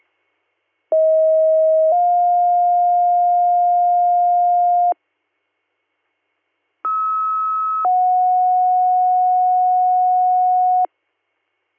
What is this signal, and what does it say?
Two-tone page, heard as a one-second tone, then a three-second tone, about 640 then 710 Hz, about 1290 then 730 Hz. It is heard from 0.9 to 10.9 s.